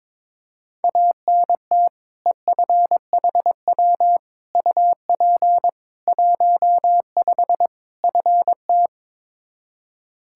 Morse code, heard as a keyed tone, about 21 wpm, 700 Hz. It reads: ANT EFHW UP 15 FT
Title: Practice QSO - Exchange 2 - Info - 22wpm